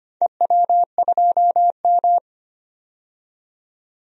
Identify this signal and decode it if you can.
Morse code, heard as a keyed tone, about 25 wpm, 700 Hz. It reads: EW2M
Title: DXCC-100 Call Signs - 4 Characters - 25wpm